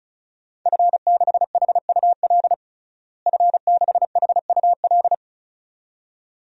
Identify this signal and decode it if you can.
Morse code, heard as a keyed tone, about 36 wpm, 700 Hz. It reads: F6HUL F6HUL
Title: DXCC-100 Call Signs - 5 Characters - Repeated 2x - 35wpm